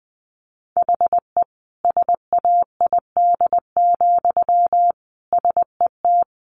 Morse code, heard as a keyed tone, about 20 wpm, 700 Hz. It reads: HE SAID, SET
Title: Sentences from Top 1400 Words 20wpm